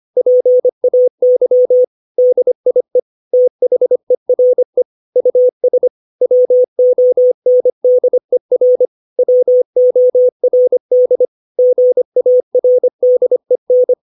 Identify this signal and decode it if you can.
Morse code, heard as a keyed tone, about 25 wpm, 500 Hz. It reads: PAY DIE THERE US WONDER WORD GARDEN
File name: 632 words / 632 Words 25WPM